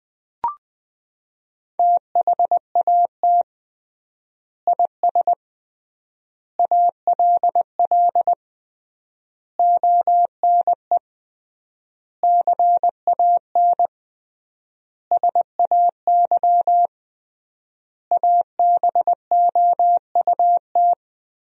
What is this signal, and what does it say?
Morse code, heard as a keyed tone, about 20 wpm, 700 Hz. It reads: THAT IS ALL ONE CAN SAY ABOUT
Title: 3x Word-Spacing - Sentences from Top 200 Words - 20wpm